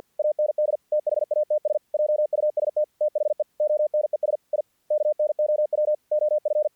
Morse code, heard as a keyed tone, about 48 wpm, 600 Hz. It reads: AND THATS JUST THE ONES I KNOW OF
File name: archived / CRSnet